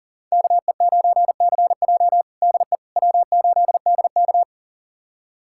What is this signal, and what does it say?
Morse code, heard as a keyed tone, about 37 wpm, 700 Hz. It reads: KE9CJ DE W8DK
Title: USA Callsign DE Callsign 40wpm